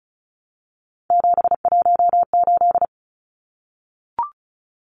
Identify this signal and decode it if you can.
Morse code, heard as a keyed tone, about 34 wpm, 700 Hz. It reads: 718
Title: Numbers - 3-Digit 35wpm